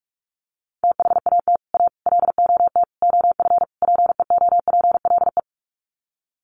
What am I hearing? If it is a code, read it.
Morse code, heard as a keyed tone, about 44 wpm, 700 Hz. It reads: THAT A LOT OF PEOPLE
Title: Sets of 5 Words 45wpm